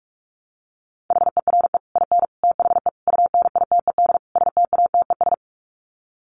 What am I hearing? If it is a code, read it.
Morse code, heard as a keyed tone, about 44 wpm, 700 Hz. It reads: HERE IN THE UNITED STATES
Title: Sets of 5 Words 45wpm